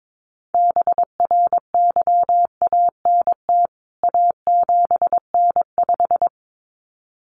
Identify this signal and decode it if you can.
Morse code, heard as a keyed tone, about 22 wpm, 700 Hz. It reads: BRYANT A7N5
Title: CWT Exchange 22wpm